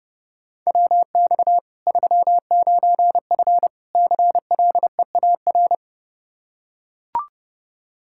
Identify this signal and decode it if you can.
Morse code, heard as a keyed tone, about 30 wpm, 700 Hz. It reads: WX 39F CLEAR
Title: Practice QSO - Exchange 2 - Info - 30wpm